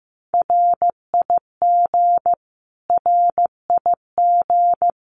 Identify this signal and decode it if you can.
Morse code, heard as a keyed tone, about 15 wpm, 700 Hz. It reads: RIG RIG
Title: Abbreviations Repeated 3x - 15wpm